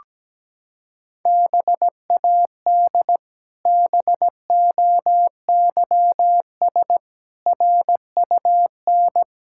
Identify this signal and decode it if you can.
Morse code, heard as a keyed tone, about 17 wpm, 700 Hz. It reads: BAD BOYS RUN
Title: Pangram Challenge v2 - 17wpm